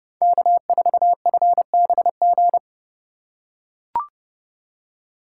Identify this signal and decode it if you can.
Morse code, heard as a keyed tone, about 30 wpm, 700 Hz. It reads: K4FBG
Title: USA Call Signs - 5 Characters - 3x Character Spacing - 30wpm